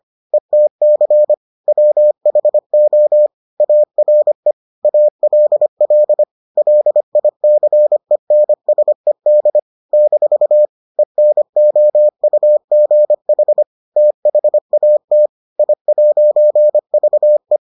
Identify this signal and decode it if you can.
Morse code, heard as a keyed tone, about 25 wpm, 600 Hz. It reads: ETC WHO ARE ALL LICENSED - ENOUGH THAT I'VE